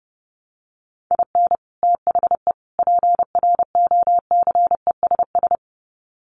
Morse code, heard as a keyed tone, about 28 wpm, 700 Hz. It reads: IN THE PROCESS
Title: Sets of 3 Words 30wpm